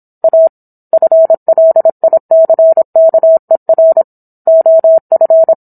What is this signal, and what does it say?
Morse code, heard as a keyed tone, about 26 wpm, 650 Hz. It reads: A FLICKER OF